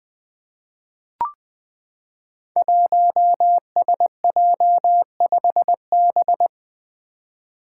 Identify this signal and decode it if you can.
Morse code, heard as a keyed tone, about 20 wpm, 700 Hz. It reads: 1SJ5B
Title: Five Challenging Letter-Number Combinations 20wpm